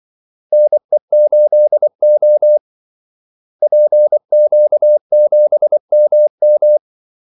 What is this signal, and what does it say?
Morse code, heard as a keyed tone, about 24 wpm, 600 Hz. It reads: NE8O PQ7MM
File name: Callsigns-24wpm